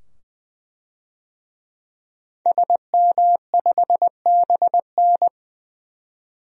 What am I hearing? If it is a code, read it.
Morse code, heard as a keyed tone, about 21 wpm, 700 Hz. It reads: SM5BN
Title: DXCC-100 Call Signs - 5 Characters - 20wpm